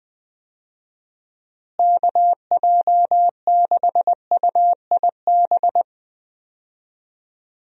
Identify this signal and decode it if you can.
Morse code, heard as a keyed tone, about 21 wpm, 700 Hz. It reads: KJ6UIB
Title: USA Call Signs - 6 Characters - Repeated 3x - 20wpm